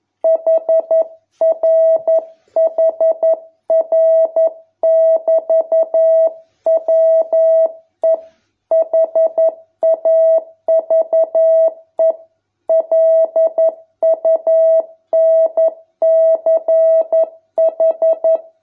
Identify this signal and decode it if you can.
Morse code, heard as a keyed tone, about 11 wpm, 650 Hz. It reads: HRHR<BT>WE HAVE LUNCH